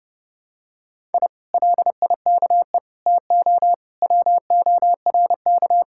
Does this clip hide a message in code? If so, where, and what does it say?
Yes: Morse code from 1.1 to 5.8 s, heard as a keyed tone, about 29 wpm, 700 Hz. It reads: I LIKE TO WORK